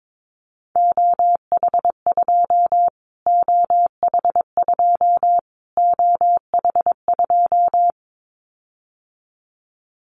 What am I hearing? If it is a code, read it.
Morse code, heard as a keyed tone, about 22 wpm, 700 Hz. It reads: OH2 OH2 OH2
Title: Callsign Prefixes for Top 25 DXCC - Repeated 3x - 22wpm